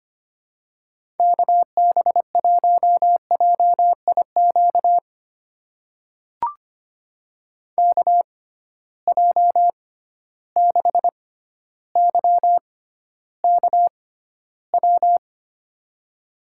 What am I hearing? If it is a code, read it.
Morse code, heard as a keyed tone, about 25 wpm, spaced as for 13 wpm, 700 Hz. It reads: KB1JIQ KJ6YKW
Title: USA Call Signs - 6 Characters - 6x Character Spacing - 25wpm